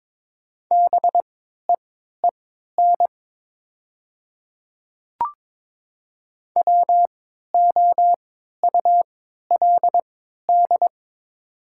Morse code, heard as a keyed tone, about 22 wpm, spaced as for 12 wpm, 700 Hz. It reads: BEEN WOULD
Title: 3x Character Spacing - Top 200 Words - 22wpm